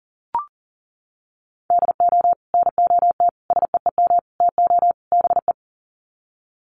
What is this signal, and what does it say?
Morse code, heard as a keyed tone, about 40 wpm, 700 Hz. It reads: DO NOT SEEM TO BE
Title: Sets of 5 Words Encore2 40wpm